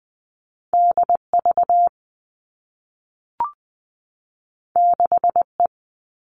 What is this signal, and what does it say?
Morse code, heard as a keyed tone, about 20 wpm, 700 Hz. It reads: D V 6 E
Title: All Cut Numbers - 2-Digit 20wpm